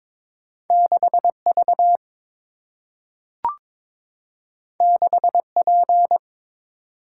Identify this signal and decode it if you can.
Morse code, heard as a keyed tone, about 22 wpm, 700 Hz. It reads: 6 V 6 P
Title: Single-Letter Single-Number 22wpm